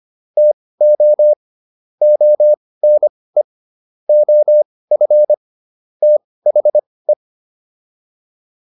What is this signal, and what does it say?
Morse code, heard as a keyed tone, about 25 wpm, spaced as for 18 wpm, 600 Hz. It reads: TO ONE OF THE